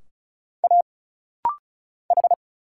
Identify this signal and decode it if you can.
Morse code, heard as a keyed tone, about 33 wpm, 700 Hz. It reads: A H
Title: Single Letters - Rapid-Fire 35wpm